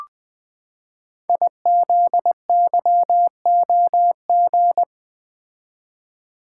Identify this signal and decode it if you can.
Morse code, heard as a keyed tone, about 20 wpm, 700 Hz. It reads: IZYOG
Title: Code Groups - 5 Letters - 20wpm